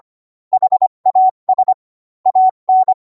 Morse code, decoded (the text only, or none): HAS AN